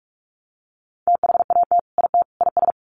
Morse code, heard as a keyed tone, about 44 wpm, 700 Hz. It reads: THAT IT IS
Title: Sets of 3 Words 45wpm